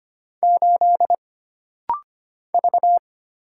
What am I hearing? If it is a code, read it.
Morse code, heard as a keyed tone, about 25 wpm, 700 Hz. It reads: 8 V
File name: Single Character (Letter:Number:Punctuation:Prosign) - Rapid-Fire - 25wpm